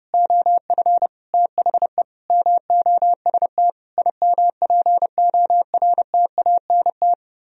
Morse code, heard as a keyed tone, about 29 wpm, 700 Hz. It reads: OF THE MOST IMPORTANT